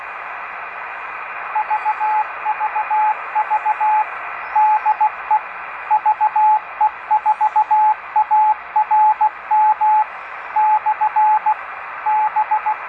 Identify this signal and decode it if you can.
Morse code, heard as a keyed tone, about 16 wpm, 860 Hz, with radio noise. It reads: VVV DE VE4ARM/B
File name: tracks / Amateur Radio Museum